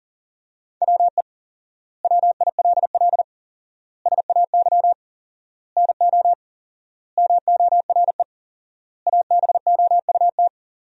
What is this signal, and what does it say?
Morse code, heard as a keyed tone, about 37 wpm, 700 Hz. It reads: WE WILL SAY NO MORE ABOUT
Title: 4x Word-Spacing - Sentences from Top 200 Words - 40wpm